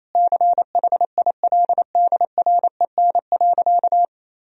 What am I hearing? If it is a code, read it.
Morse code, heard as a keyed tone, about 28 wpm, 700 Hz. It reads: CHILDREN.
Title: Sentences from Top 300 Words and Abbreviations 28wpm